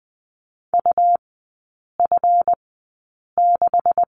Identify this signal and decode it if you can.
Morse code, heard as a keyed tone, about 20 wpm, spaced as for 8 wpm, 700 Hz. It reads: UF6
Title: 218 Lesson 35e - Number 0 - Three Characters 20wpm